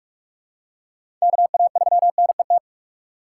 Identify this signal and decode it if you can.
Morse code, heard as a keyed tone, about 46 wpm, 700 Hz. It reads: KA3NET
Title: USA Call Signs - 6 Characters - 6x Character Spacing - 45wpm